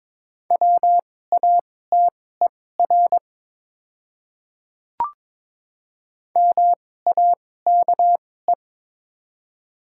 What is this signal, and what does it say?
Morse code, heard as a keyed tone, about 22 wpm, spaced as for 16 wpm, 700 Hz. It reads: WATER MAKE
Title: 2x Character Spacing - Top 100 Words - 22wpm